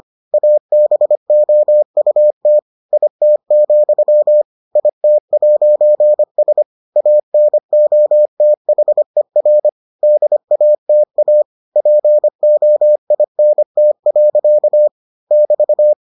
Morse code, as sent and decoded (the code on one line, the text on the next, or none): .- -... --- ..- - / .. - --..-- / .. - .----. ... / .- -. --- - .... .-. / -.. .- - .- / .--. --- .. -. - .-.-.- / -...-
ABOUT IT, IT'S ANOTHER DATA POINT. <BT>